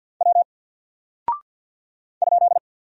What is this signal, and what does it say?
Morse code, heard as a keyed tone, about 48 wpm, 700 Hz. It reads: W ?
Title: Single Character (Letter:Number:Punctuation:Prosign) Alt 50wpm